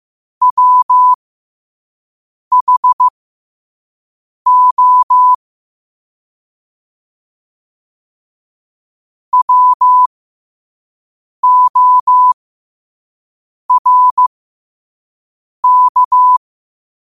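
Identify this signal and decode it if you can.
Morse code, heard as a keyed tone, about 15 wpm, spaced as for 5 wpm, 1000 Hz. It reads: WHO WORK
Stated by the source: Morse code sent at 5 wpm.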